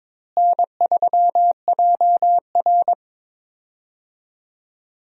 Morse code, heard as a keyed tone, about 22 wpm, 700 Hz. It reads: N3JR
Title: USA Call Signs - 4 Characters - Repeated 3x - 22wpm